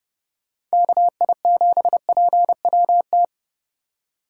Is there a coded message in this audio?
Yes: Morse code from 0.7 to 3.3 s, heard as a keyed tone, about 34 wpm, 700 Hz. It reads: KI7PWT